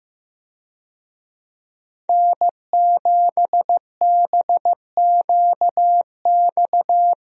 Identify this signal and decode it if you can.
Morse code, heard as a keyed tone, about 16 wpm, 700 Hz. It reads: N7BQX
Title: USA Call Signs - 5 Characters - Repeated 3x - 15wpm